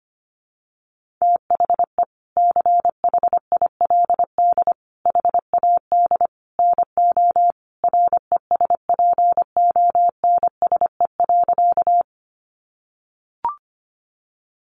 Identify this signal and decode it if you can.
Morse code, heard as a keyed tone, about 25 wpm, 700 Hz. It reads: THE CHILD HAD NO RESPONSE.
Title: Sentences from Top 1400 Words 25wpm